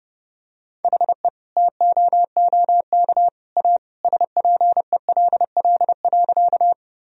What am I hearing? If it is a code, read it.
Morse code, heard as a keyed tone, about 30 wpm, 700 Hz. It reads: HE TOOK A SPELL.
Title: Sentences from Top 1000 Words - Review Part 1 - 30wpm